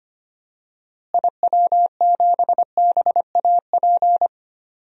Morse code, heard as a keyed tone, about 25 wpm, 700 Hz. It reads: IW7BAP
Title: DXCC-100 Call Signs - 6 Characters - Repeated 3x - 25wpm